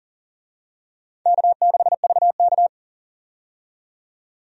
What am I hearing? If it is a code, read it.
Morse code, heard as a keyed tone, about 40 wpm, 700 Hz. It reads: K6VK
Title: USA Call Signs - 4 Characters - 40wpm